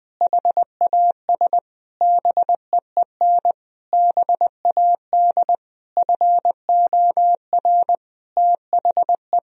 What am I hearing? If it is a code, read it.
Morse code, heard as a keyed tone, about 20 wpm, 700 Hz. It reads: HAS BEEN BAD FOR THE
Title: Sentences from Top 1000 Words - Review Part 3 - 20wpm